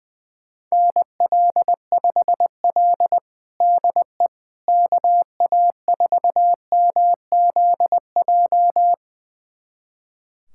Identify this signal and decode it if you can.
Morse code, heard as a keyed tone, about 20 wpm, 700 Hz. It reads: NL5L DE KA4MZJ
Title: USA Callsign DE Callsign 20wpm